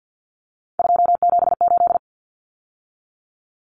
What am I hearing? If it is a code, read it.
Morse code, heard as a keyed tone, about 50 wpm, 700 Hz. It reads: 278
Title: Numbers - 3-Digit 50wpm